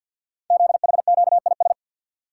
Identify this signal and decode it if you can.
Morse code, heard as a keyed tone, about 50 wpm, 700 Hz. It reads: ZHQIS
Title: Code Groups - 5 Letters - 50wpm